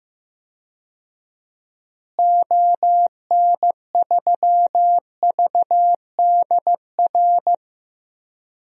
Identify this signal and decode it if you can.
Morse code, heard as a keyed tone, about 15 wpm, 700 Hz. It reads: ON3VDR
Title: DXCC-100 Call Signs - 6 Characters - Repeated 2x - 15wpm